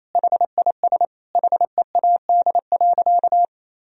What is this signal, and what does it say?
Morse code, heard as a keyed tone, about 28 wpm, 700 Hz. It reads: HIS HEAD.
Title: Sentences from Top 1000 Words - Review Part 4 - 28wpm